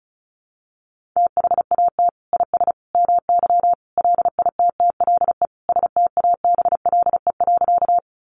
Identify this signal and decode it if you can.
Morse code, heard as a keyed tone, about 34 wpm, 700 Hz. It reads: THAT IS MY LITTLE STABLE.
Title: Sentences from Top 1500 Words 35wpm